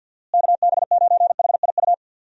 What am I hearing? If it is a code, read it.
Morse code, heard as a keyed tone, about 46 wpm, 700 Hz. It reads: KB9HIU
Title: USA Call Signs - 6 Characters - Repeated 3x - 50wpm